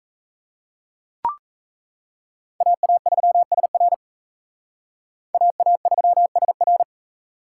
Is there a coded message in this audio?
Yes: Morse code from 2.6 to 6.8 s, heard as a keyed tone, about 35 wpm, 700 Hz. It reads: AA3SR AA3SR